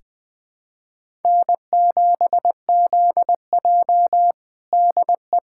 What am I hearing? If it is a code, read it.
Morse code, heard as a keyed tone, about 21 wpm, 700 Hz. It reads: N7ZJ DE
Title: USA Callsign DE Callsign - 4 Characters - 20wpm